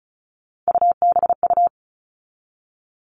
Morse code, heard as a keyed tone, about 34 wpm, 700 Hz. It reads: UBU
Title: All Cut Numbers - 3-Digit 35wpm